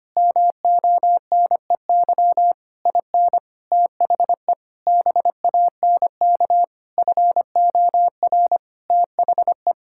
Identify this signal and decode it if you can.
Morse code, heard as a keyed tone, about 25 wpm, 700 Hz. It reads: MONEY IN THE BANK FOR THE
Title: Sentences from Top 1000 Words - Review Part 6 - 25wpm